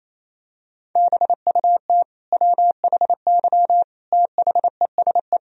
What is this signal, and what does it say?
Morse code, heard as a keyed tone, about 28 wpm, 700 Hz. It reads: BUT WHY THESE
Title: Sentences from Top 1000 Words - Review Part 1 - 28wpm